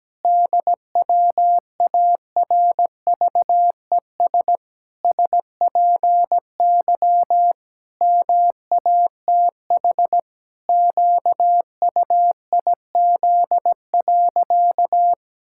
Morse code, heard as a keyed tone, about 17 wpm, 700 Hz. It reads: DWARVES SPY MATH QUIZ.
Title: Pangram Challenge v2 - 17wpm